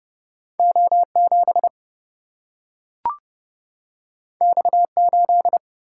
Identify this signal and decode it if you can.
Morse code, heard as a keyed tone, about 30 wpm, 700 Hz. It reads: O 7 X 8
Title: Single-Letter Single-Number 30wpm